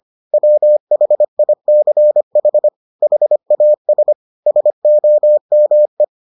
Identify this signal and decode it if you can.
Morse code, heard as a keyed tone, about 25 wpm, 600 Hz. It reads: WHICH HAS SOME